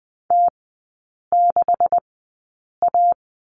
Morse code, heard as a keyed tone, about 20 wpm, spaced as for 8 wpm, 690 Hz. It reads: T6A
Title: 191 Lesson 31e - Letter v - Three Characters 20wpm